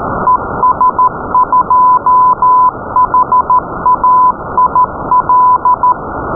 Morse code, heard as a keyed tone, about 13 wpm, 1010 Hz, with noise. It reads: ES2HAIL
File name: py4zbz / zbz / es2hail beaconCW